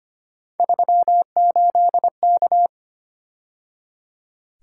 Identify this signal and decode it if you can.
Morse code, heard as a keyed tone, about 29 wpm, 700 Hz. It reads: 38K
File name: Speed-Racing - Three Letter-Number Combinations - 25wpm